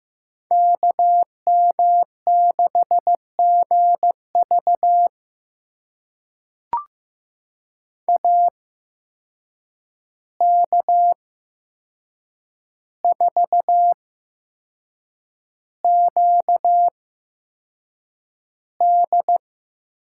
Morse code, heard as a keyed tone, about 15 wpm, spaced as for 6 wpm, 700 Hz. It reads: KM6GV AK4QD
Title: USA Call Signs - 5 Characters - 8x Character Spacing - 15wpm